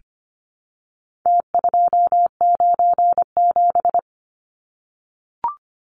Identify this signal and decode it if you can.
Morse code, heard as a keyed tone, about 25 wpm, 700 Hz. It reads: T297
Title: Numbers - Contest Style - 4-Digit 25wpm